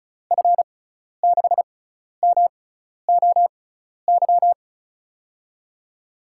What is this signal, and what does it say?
Morse code, heard as a keyed tone, about 35 wpm, spaced as for 12 wpm, 700 Hz. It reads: F6MOY
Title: USA Call Signs - 6 Characters - 6x Character Spacing - 35wpm